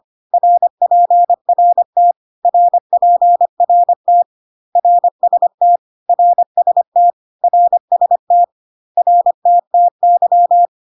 Morse code, heard as a keyed tone, about 25 wpm, 700 Hz. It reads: RPRT RPRT RST RST RST RTTY
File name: cw / CW common abbreviations alphabetical repeated 3 times 25 25wpm ik1hge